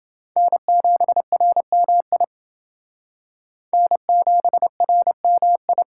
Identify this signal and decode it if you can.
Morse code, heard as a keyed tone, about 28 wpm, 700 Hz. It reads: N7RMI N7RMI